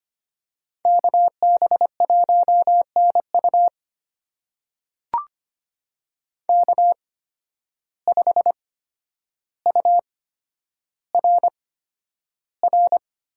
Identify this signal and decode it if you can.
Morse code, heard as a keyed tone, about 25 wpm, spaced as for 11 wpm, 700 Hz. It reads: KB1NU K5URR
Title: USA Call Signs - 5 Characters - 8x Character Spacing - 25wpm